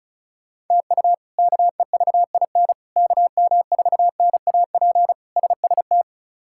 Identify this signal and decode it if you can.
Morse code, heard as a keyed tone, about 36 wpm, 700 Hz. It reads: TU KEVIN KM4NAP SST
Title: K1USN SST QSO Practice - 35wpm